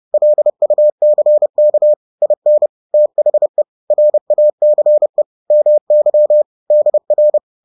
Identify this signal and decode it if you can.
Morse code, heard as a keyed tone, about 30 wpm, 600 Hz. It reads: LUCK IN THE RACE MY DR
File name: 30wpm-7